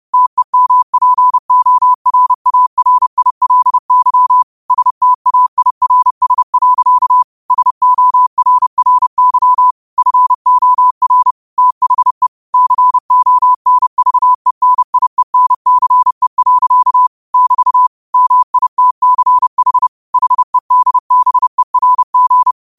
Morse code, heard as a keyed tone, about 30 wpm, 1000 Hz. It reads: TEMPORARILY STAIRS. SORRY FOR THE CONVENIENCE. <BT> MITCH HEDBERG
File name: qotdfiles / 30wpm